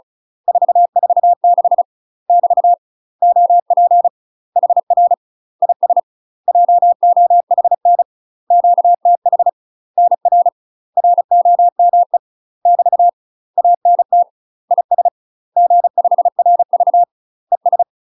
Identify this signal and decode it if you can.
Morse code, heard as a keyed tone, about 34 wpm, 700 Hz. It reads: 446 <BT> OP HR IS JOHN QTH NR ROME <BT> ANT IS G5RV ES
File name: UK MASTER QSOs TRAINING / 35WPM / UK MASTER QSOs 0008